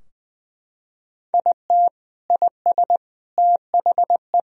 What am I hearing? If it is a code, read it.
Morse code, heard as a keyed tone, about 20 wpm, 700 Hz. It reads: IT IS THE
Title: Sentences from Top 1000 Words - Review Part 2 - 20wpm